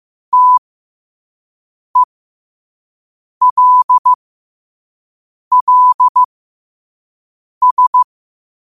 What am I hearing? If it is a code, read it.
Morse code, heard as a keyed tone, about 15 wpm, spaced as for 5 wpm, 1000 Hz. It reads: TELLS
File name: 5wpm